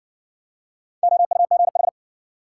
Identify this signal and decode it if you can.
Morse code, heard as a keyed tone, about 56 wpm, 700 Hz. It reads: OUGH